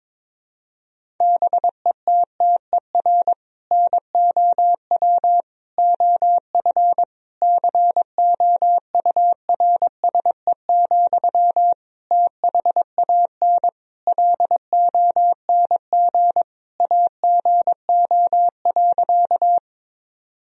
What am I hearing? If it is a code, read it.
Morse code, heard as a keyed tone, about 22 wpm, 700 Hz. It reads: BETTER NOW OF COURSE, THAN LONG AGO.
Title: Sentences from Top 1000 Words - Review Part 3 - 22wpm